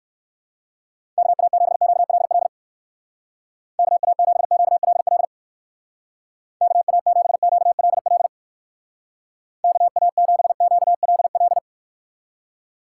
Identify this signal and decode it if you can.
Morse code, heard as a keyed tone, about 57 wpm, 700 Hz. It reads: KA7QLL KA7QLL KA7QLL KA7QLL